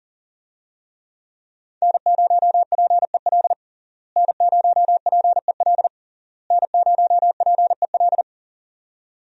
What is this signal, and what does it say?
Morse code, heard as a keyed tone, about 37 wpm, 700 Hz. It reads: N0PEL N0PEL N0PEL
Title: USA Call Signs - 5 Characters - Repeated 3x - 40wpm